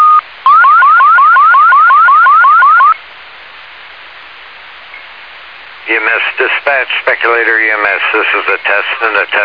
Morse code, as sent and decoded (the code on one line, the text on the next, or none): none